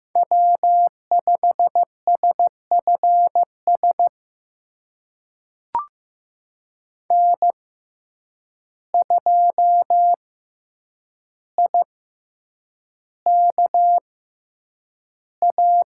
Morse code, heard as a keyed tone, about 15 wpm, spaced as for 8 wpm, 700 Hz. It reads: W5SFS N2IKA
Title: USA Call Signs - 5 Characters - 6x Character Spacing - 15wpm